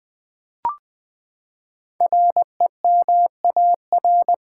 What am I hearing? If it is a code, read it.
Morse code, heard as a keyed tone, about 20 wpm, 700 Hz. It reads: REMAR
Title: Build A Word - 5 Letters At A Time - 10-Letter Words - 20wpm